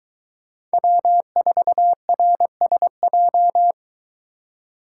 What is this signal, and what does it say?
Morse code, heard as a keyed tone, about 23 wpm, 700 Hz. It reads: W4RSJ